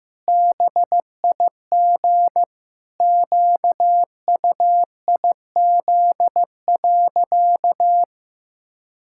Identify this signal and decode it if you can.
Morse code, heard as a keyed tone, about 16 wpm, 700 Hz. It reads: BIG QUIZ.
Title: Pangram Challenge v2 - 15wpm